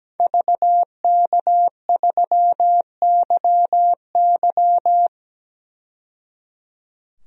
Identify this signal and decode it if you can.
Morse code, heard as a keyed tone, about 17 wpm, 700 Hz. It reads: VK3YY
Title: DXCC-100 Call Signs - 5 Characters - 17wpm